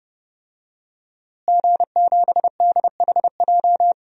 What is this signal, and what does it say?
Morse code, heard as a keyed tone, about 30 wpm, 700 Hz. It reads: G7DHJ
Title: DXCC-100 Call Signs - 5 Characters - Repeated 3x - 30wpm